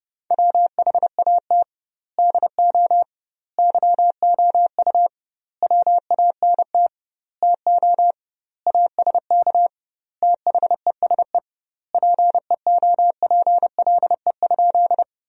Morse code, heard as a keyed tone, about 30 wpm, 700 Hz. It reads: WHAT DO YOU WANT TO ASK THESE PEOPLE?
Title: 2x Word-Spacing - Sentences from Top 200 Words - 30wpm